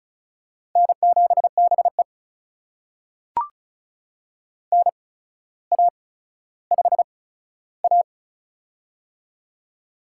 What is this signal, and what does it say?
Morse code, heard as a keyed tone, about 36 wpm, spaced as for 15 wpm, 700 Hz. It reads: N7BE NA5A
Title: USA Call Signs - 4 Characters - 8x Character Spacing - 35wpm